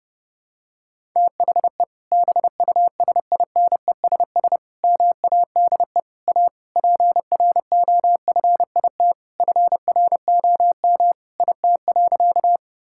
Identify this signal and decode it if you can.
Morse code, heard as a keyed tone, about 30 wpm, 700 Hz. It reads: THE BUSINESS MADE A PROFIT FROM IT.
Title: Sentences from Top 1000 Words - Review Part 5 - 30wpm